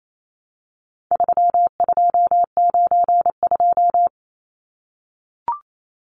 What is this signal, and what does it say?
Morse code, heard as a keyed tone, about 28 wpm, 700 Hz. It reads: 3292
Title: Numbers - 4-Digit 28wpm